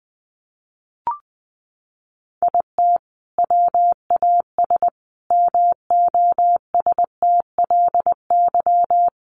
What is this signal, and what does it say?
Morse code, heard as a keyed tone, about 19 wpm, 700 Hz. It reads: IT WAS MOSTLY
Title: Sentences from Top 1300 Words 20wpm